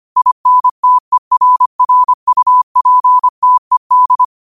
Morse code, heard as a keyed tone, about 25 wpm, 1000 Hz. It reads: INTERRUPTED
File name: qotdfiles / 25wpm